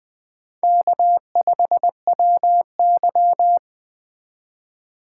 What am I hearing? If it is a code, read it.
Morse code, heard as a keyed tone, about 20 wpm, 700 Hz. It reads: K5WY
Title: Speed Racing - USA 1x2 Call Signs 17wpm